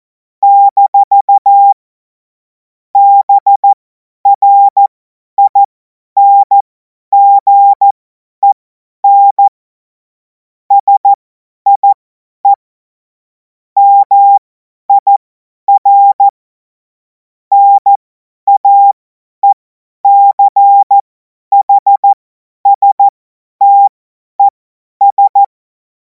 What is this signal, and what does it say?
Morse code, heard as a keyed tone, about 14 wpm, spaced as for 10 wpm, 800 Hz. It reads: - BRINGEN SIE MIR NAECHSTES
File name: 50 F70